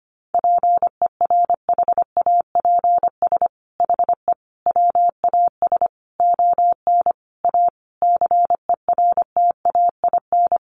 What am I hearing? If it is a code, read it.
Morse code, heard as a keyed tone, about 25 wpm, 700 Hz. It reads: PERHAPS HE WAS ON A CERTAIN
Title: Sentences from Top 1100 Words 25wpm